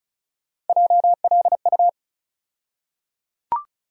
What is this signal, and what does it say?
Morse code, heard as a keyed tone, about 36 wpm, 700 Hz. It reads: JLU
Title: USA Call Sign Postfixes - 3 Letters 35wpm